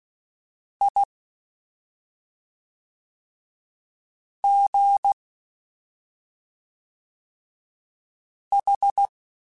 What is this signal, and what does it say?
Morse code, heard as a keyed tone, about 16 wpm, 800 Hz. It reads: I G H
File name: Special INGs / Special INGs-mp3s / SlowToFaster ing @5